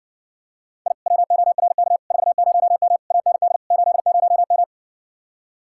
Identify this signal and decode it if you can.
Morse code, heard as a keyed tone, about 60 wpm, 700 Hz. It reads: I WORK 40M AND 80M